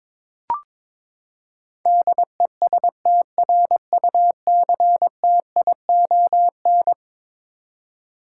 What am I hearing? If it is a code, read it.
Morse code, heard as a keyed tone, about 22 wpm, 700 Hz. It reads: DESTRUCTION
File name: practice / 22wpm / Eleven Letter Words - 22wpm